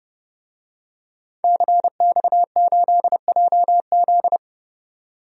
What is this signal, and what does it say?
Morse code, heard as a keyed tone, about 32 wpm, 700 Hz. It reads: CX8JZ
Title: DXCC-100 Call Signs - 5 Characters - 30wpm